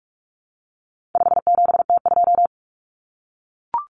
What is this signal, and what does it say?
Morse code, heard as a keyed tone, about 44 wpm, 700 Hz. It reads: 57T2
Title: Common Cut Numbers - 4-Digit 45wpm